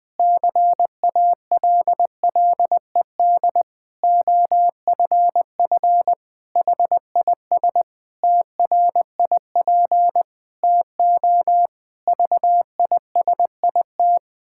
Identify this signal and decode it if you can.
Morse code, heard as a keyed tone, about 20 wpm, 700 Hz. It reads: CALLED OFF HIS TRIP TO VISIT
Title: Pangram Challenge v2 - 20wpm